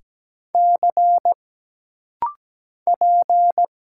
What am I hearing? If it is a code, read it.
Morse code, heard as a keyed tone, about 17 wpm, 700 Hz. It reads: C P
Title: Single Letters - Rapid-Fire 17wpm